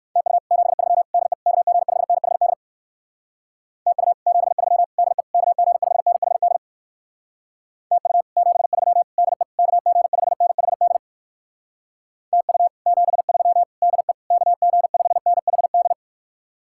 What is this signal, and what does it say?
Morse code, heard as a keyed tone, about 59 wpm, 700 Hz. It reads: TU 73 DE KG5NHD TU 73 DE KG5NHD TU 73 DE KG5NHD TU 73 DE KG5NHD